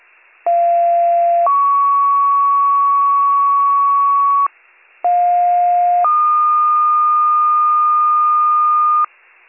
Two-tone page, heard as a one-second tone, then a three-second tone, about 690 then 1120 Hz, about 710 then 1220 Hz. It is heard from 0.5 to 9.1 s.